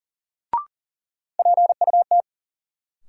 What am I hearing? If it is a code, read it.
Morse code, heard as a keyed tone, about 37 wpm, 700 Hz. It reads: PUT
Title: Sentences from Top 200 Words - New Word Drill - 40wpm